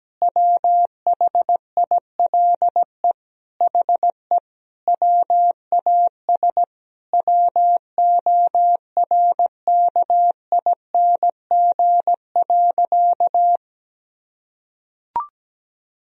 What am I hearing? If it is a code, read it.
Morse code, heard as a keyed tone, about 17 wpm, 700 Hz. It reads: WHILE HE WAS WORKING.
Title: Sentences from Top 1000 Words - Review Part 4 - 17wpm